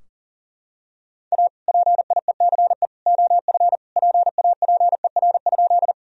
Morse code, heard as a keyed tone, about 40 wpm, 700 Hz. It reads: A PIECE OF PAPER?
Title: Sentences from Top 1000 Words - Review Part 2 - 40wpm